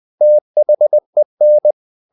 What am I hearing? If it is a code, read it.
Morse code, heard as a keyed tone, about 20 wpm, 600 Hz. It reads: THEN
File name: the old man and the sea / the old man and the sea part 1 20wpm